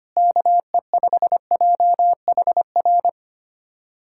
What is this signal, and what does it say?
Morse code, heard as a keyed tone, about 25 wpm, 700 Hz. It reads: KE5JHR
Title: USA Callsign DE Callsign - 6 Characters - 25wpm